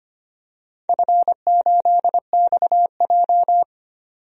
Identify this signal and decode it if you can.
Morse code, heard as a keyed tone, about 25 wpm, 700 Hz. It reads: F8XJ
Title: DXCC-100 Call Signs - 4 Characters - 25wpm